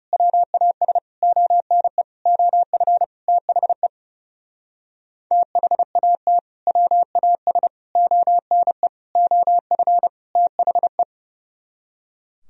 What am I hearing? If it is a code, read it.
Morse code, heard as a keyed tone, about 28 wpm, 700 Hz. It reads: WAS ONE OF THE THAT WAS ONE OF THE
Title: Speed-Racing - Sets of 5 Words 30wpm